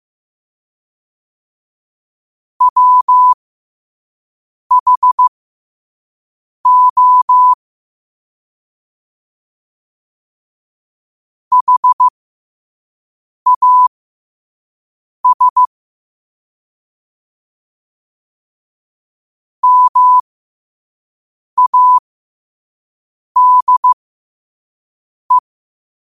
Morse code, heard as a keyed tone, about 15 wpm, spaced as for 5 wpm, 1000 Hz. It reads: WHO HAS MADE
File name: qotdfiles / 5wpm